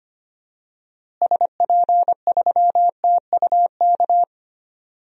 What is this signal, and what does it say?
Morse code, heard as a keyed tone, about 25 wpm, 700 Hz. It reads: SP3TUK
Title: DXCC-100 Call Signs - 6 Characters - Repeated 3x - 25wpm